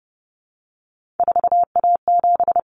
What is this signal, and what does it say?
Morse code, heard as a keyed tone, about 30 wpm, 700 Hz. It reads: 4A7
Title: Common Cut Numbers - 3-Digit 30wpm